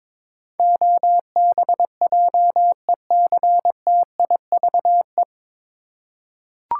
Morse code, heard as a keyed tone, about 22 wpm, 700 Hz. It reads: OBJECTIVE
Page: Morse Code Ninja practice 22wpm